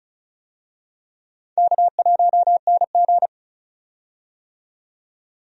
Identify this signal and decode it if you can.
Morse code, heard as a keyed tone, about 34 wpm, 700 Hz. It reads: K1NG
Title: USA Call Signs - 4 Characters - 35wpm